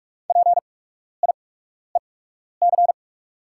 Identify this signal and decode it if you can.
Morse code, heard as a keyed tone, about 44 wpm, spaced as for 12 wpm, 700 Hz. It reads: PIEC
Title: 8x Character Spacing - Top 500 Words - 45wpm